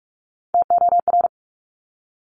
Morse code, heard as a keyed tone, about 46 wpm, 700 Hz. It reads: TOR
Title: Top Three Letter Combinations in Words 45wpm